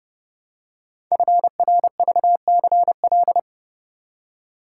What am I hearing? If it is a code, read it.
Morse code, heard as a keyed tone, about 29 wpm, 700 Hz. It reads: FRVCL
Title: Code Groups - 5 Letters - 30wpm